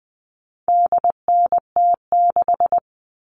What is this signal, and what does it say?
Morse code, heard as a keyed tone, about 20 wpm, 700 Hz. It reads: DNT6
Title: All Cut Numbers - 4-Digit 20wpm